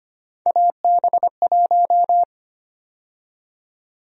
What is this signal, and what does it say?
Morse code, heard as a keyed tone, about 25 wpm, 700 Hz. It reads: AB1
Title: USA Call Sign Prefixes - 25wpm